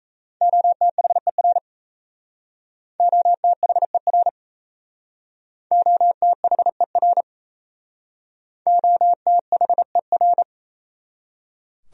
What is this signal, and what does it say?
Morse code, heard as a keyed tone, about 34 wpm, 700 Hz. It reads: OTHER OTHER OTHER OTHER
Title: Speed-Racing Top 100 Words 28wpm